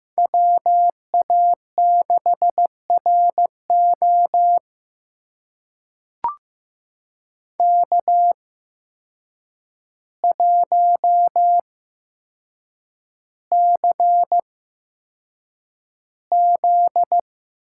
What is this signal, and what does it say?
Morse code, heard as a keyed tone, about 15 wpm, spaced as for 7 wpm, 700 Hz. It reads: WA6RO K1CZ
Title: USA Call Signs - 8x Character Spacing - 15wpm